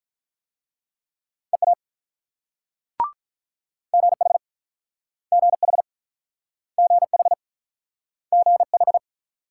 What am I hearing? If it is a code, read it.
Morse code, heard as a keyed tone, about 46 wpm, 690 Hz. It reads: EA GH GH GH GH